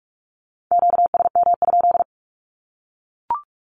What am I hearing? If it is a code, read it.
Morse code, heard as a keyed tone, about 44 wpm, 700 Hz. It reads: QSM?
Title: Q-Codes 45wpm